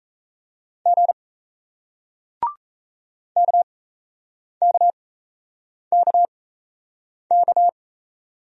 Morse code, heard as a keyed tone, about 38 wpm, spaced as for 8 wpm, 700 Hz. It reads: G KKKK